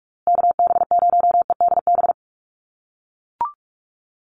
Morse code, heard as a keyed tone, about 44 wpm, 700 Hz. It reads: KB0EDB
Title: USA Call Signs Repeated 3x 45wpm